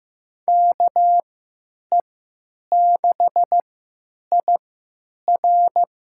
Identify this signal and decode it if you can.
Morse code, heard as a keyed tone, about 15 wpm, spaced as for 9 wpm, 700 Hz. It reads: KE6IR
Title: USA Call Signs - 5 Characters - 3x Character Spacing - 15wpm